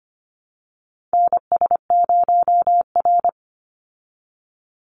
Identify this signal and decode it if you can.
Morse code, heard as a keyed tone, about 25 wpm, 700 Hz. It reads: NS0R
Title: Call Signs Repeated 2x - 25wpm